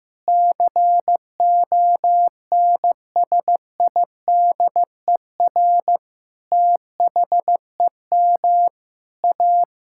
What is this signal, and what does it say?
Morse code, heard as a keyed tone, about 15 wpm, 700 Hz. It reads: CONSIDER THEM A
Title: Sentences from Top 1000 Words - Review Part 5 - 15wpm